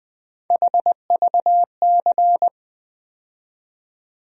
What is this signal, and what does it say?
Morse code, heard as a keyed tone, about 20 wpm, 700 Hz. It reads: HVC
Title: USA Call Sign Postfixes - 3 Letters 20wpm